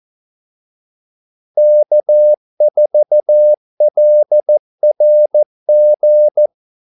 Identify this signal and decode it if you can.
Morse code, heard as a keyed tone, about 14 wpm, 600 Hz. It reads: K4LRG
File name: Callsigns-14wpm